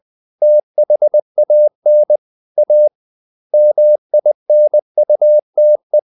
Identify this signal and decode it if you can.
Morse code, heard as a keyed tone, about 20 wpm, 600 Hz. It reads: THAN A MINUTE
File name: the old man and the sea / the old man and the sea part 1 20wpm